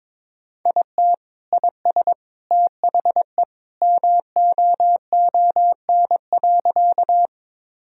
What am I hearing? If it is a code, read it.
Morse code, heard as a keyed tone, about 22 wpm, 700 Hz. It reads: IT IS THE MOON.
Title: Sentences from Top 1000 Words - Review Part 3 - 22wpm